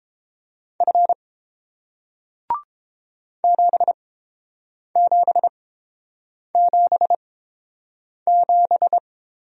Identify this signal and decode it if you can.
Morse code, heard as a keyed tone, about 30 wpm, spaced as for 7 wpm, 700 Hz. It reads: F 7777